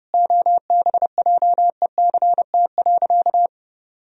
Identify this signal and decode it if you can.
Morse code, heard as a keyed tone, about 32 wpm, 700 Hz. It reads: OBJECT.
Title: Sentences from Top 1000 Words - Review Part 3 - 30wpm